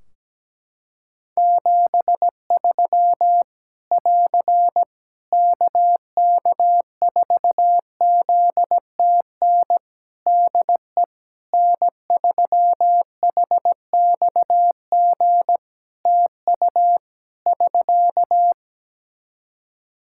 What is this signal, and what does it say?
Morse code, heard as a keyed tone, about 17 wpm, 700 Hz. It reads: 73 <AR> KK4ZTN DE N3HXG TU <SK>
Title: Practice QSO - Exchange 4 - Ending - 17wpm